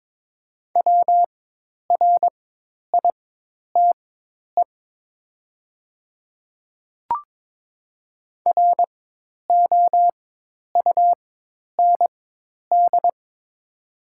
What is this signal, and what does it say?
Morse code, heard as a keyed tone, about 22 wpm, spaced as for 10 wpm, 700 Hz. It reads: WRITE ROUND
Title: 4x Character Spacing - Top 300 Words - 22wpm